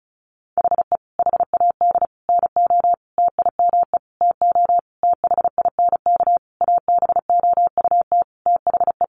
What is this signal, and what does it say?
Morse code, heard as a keyed tone, about 34 wpm, 700 Hz. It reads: HE HAD NO TIME TO THINK ABOUT THE
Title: Sentences from Top 700 Words 35wpm